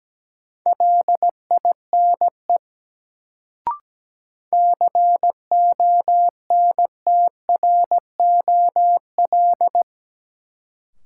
Morse code, heard as a keyed tone, about 17 wpm, 700 Hz. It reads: LINE CONTROL
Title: Sentences from Top 1200 Words - New Word Drill Alt - 17wpm